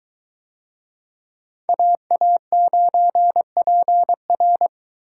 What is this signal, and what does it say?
Morse code, heard as a keyed tone, about 22 wpm, 700 Hz. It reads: AA9PR